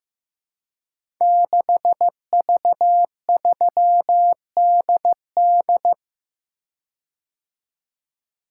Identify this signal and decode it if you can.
Morse code, heard as a keyed tone, about 15 wpm, 700 Hz. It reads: 6V3DD
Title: Code Groups - 5 Letter:Numbers - 15wpm